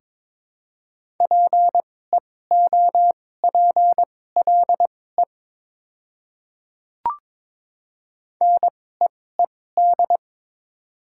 Morse code, heard as a keyed tone, about 22 wpm, spaced as for 16 wpm, 700 Hz. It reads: PEOPLE NEED
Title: 2x Character Spacing - Top 200 Words - 22wpm